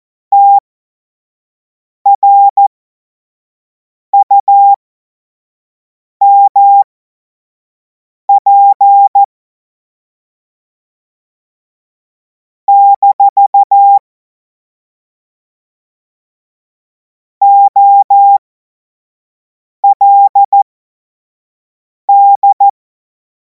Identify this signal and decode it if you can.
Morse code, heard as a keyed tone, about 14 wpm, spaced as for 5 wpm, 800 Hz. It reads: TRUMP - OLD
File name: Morsekurs / Witze / Witz 30. Juni 2026 25 F70